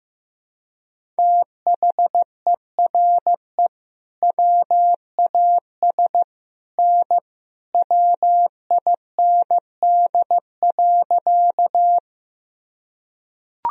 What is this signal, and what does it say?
Morse code, heard as a keyed tone, about 15 wpm, 700 Hz. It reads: THERE WAS N WIND.
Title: Sentences from Top 400 Words and Abbreviations 15wpm